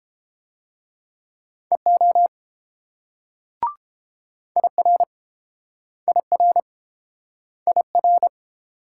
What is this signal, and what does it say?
Morse code, heard as a keyed tone, about 30 wpm, 700 Hz. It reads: EO IR IR IR